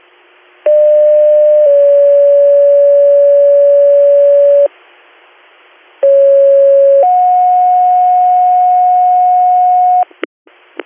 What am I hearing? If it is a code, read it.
Two-tone page, heard as a one-second tone, then a three-second tone, about 600 then 570 Hz, about 550 then 730 Hz. It is heard from 0.7 to 10.0 s.